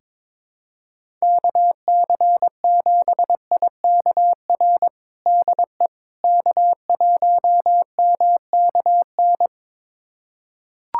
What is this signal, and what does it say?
Morse code, heard as a keyed tone, about 22 wpm, 700 Hz. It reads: KC7IKR DE K1MKN
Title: USA Callsign DE Callsign 22wpm